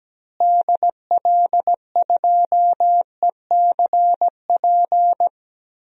Morse code, heard as a keyed tone, about 17 wpm, 700 Hz. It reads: DL2ECP
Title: DXCC-100 Call Signs - 6 Characters - Repeated 3x - 17wpm